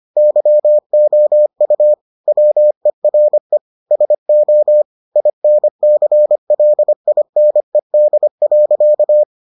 Morse code, heard as a keyed tone, about 25 wpm, 600 Hz. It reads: YOU WERE SO INCLINED.